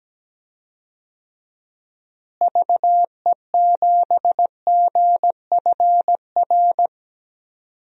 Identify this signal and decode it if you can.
Morse code, heard as a keyed tone, about 17 wpm, 700 Hz. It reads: VE7GFR
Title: DXCC-100 Call Signs - 6 Characters - Repeated 3x - 17wpm